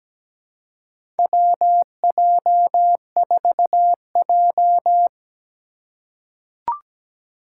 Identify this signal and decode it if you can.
Morse code, heard as a keyed tone, about 17 wpm, 700 Hz. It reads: WJ4J
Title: USA Call Signs - 4 Characters - 4x Character Spacing - 17wpm